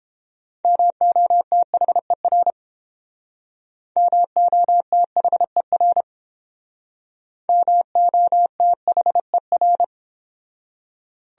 Morse code, heard as a keyed tone, about 30 wpm, 700 Hz. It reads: MOTHER MOTHER MOTHER